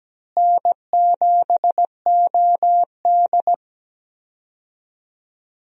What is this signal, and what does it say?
Morse code, heard as a keyed tone, about 17 wpm, 700 Hz. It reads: N7OD
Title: USA Call Signs - 4 Characters - Repeated 3x - 17wpm